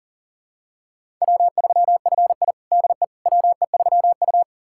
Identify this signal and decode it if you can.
Morse code, heard as a keyed tone, about 37 wpm, 700 Hz. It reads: W3FI DE WE3U
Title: USA Callsign DE Callsign - 4 Characters - 40wpm